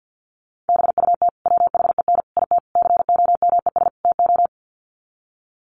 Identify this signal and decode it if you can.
Morse code, heard as a keyed tone, about 50 wpm, 700 Hz. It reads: BUT WHEN IT COMES TO
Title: Sets of 5 Words 50wpm